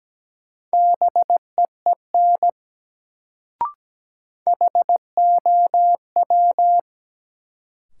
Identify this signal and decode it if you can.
Morse code, heard as a keyed tone, about 17 wpm, 700 Hz. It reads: BEEN HOW
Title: Sentences from Top 100 Words - New Word Drill - 17wpm